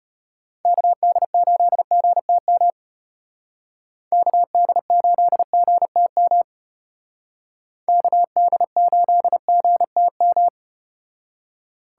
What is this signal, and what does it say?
Morse code, heard as a keyed tone, about 33 wpm, 700 Hz. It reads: KD8GTM KD8GTM KD8GTM